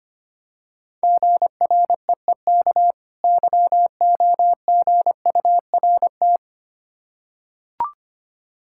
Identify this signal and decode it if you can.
Morse code, heard as a keyed tone, about 25 wpm, 700 Hz. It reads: GREEK YOGURT
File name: practice / 25wpm / Difficult Words with Hints - 25wpm